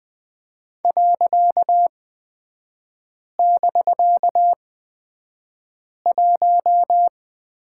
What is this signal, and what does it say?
Morse code, heard as a keyed tone, about 20 wpm, spaced as for 5 wpm, 700 Hz. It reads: .<BK>1
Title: 255 Lesson 41e - Period - Three Characters 20wpm